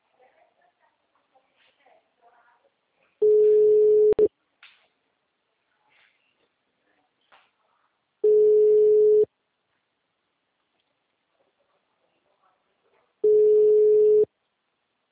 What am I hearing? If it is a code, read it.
Ringback tone (Europe, 425 Hz), with speech.